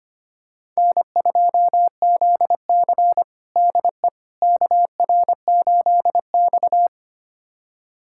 Morse code, heard as a keyed tone, about 25 wpm, 700 Hz. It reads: N2ZC DE KR8X
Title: USA Callsign DE Callsign - 4 Characters - 25wpm